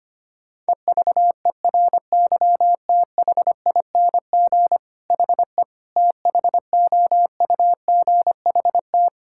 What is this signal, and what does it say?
Morse code, heard as a keyed tone, about 25 wpm, 700 Hz. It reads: EVERYTHING HE THOUGHT
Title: Sentences from Top 1000 Words - Review Part 6 - 25wpm